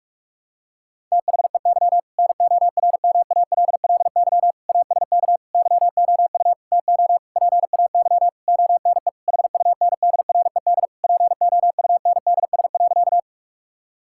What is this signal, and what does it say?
Morse code, heard as a keyed tone, about 46 wpm, 700 Hz. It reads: THEY NORMALLY ASK YOU TO PAY ONE HUNDRED POUNDS.
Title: Sentences from Top 1000 Words - Review Part 6 - 45wpm